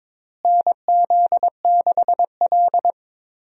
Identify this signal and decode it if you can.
Morse code, heard as a keyed tone, about 22 wpm, 700 Hz. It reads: NZ6L
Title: USA Call Signs - 4 Characters - Repeated 2x - 22wpm